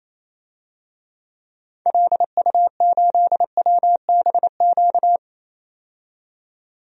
Morse code, heard as a keyed tone, about 28 wpm, 700 Hz. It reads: LU8WBQ
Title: DXCC-100 Call Signs - 6 Characters - 28wpm